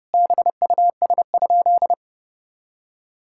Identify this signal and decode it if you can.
Morse code, heard as a keyed tone, about 30 wpm, 700 Hz. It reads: BUS?
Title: Sentences from Top 1000 Words - Review Part 6 - 30wpm